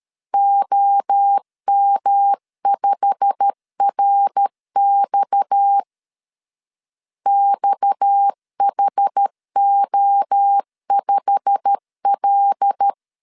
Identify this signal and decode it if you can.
Morse code, heard as a keyed tone, about 13 wpm, 800 Hz. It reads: OM5RX XHO5L
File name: LEZIONI / 8 YJ45 / 8 YJ45 12wpm 2151